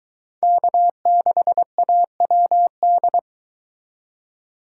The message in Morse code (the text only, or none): K6AWD